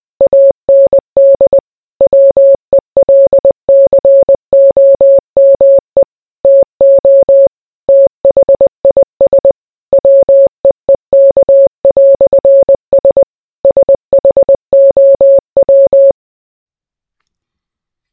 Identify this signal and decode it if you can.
Morse code, heard as a keyed tone, about 20 wpm, 550 Hz. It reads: AND WELCOME TO THIS WEEK"S SHOW